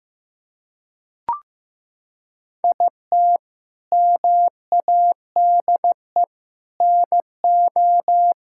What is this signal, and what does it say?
Morse code, heard as a keyed tone, about 16 wpm, 700 Hz. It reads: IT MADE NO
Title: Sentences from Top 1000 Words - Review Part 5 - 15wpm